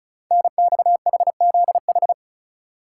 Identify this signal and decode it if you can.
Morse code, heard as a keyed tone, about 36 wpm, 690 Hz. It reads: NXHZH
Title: Code Groups - 5 Letters - 35wpm